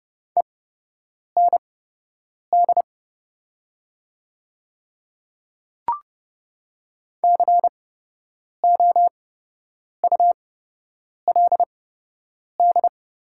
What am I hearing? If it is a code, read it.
Morse code, heard as a keyed tone, about 30 wpm, spaced as for 8 wpm, 700 Hz. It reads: END COULD
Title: 8x Character Spacing - Top 200 Words - 30wpm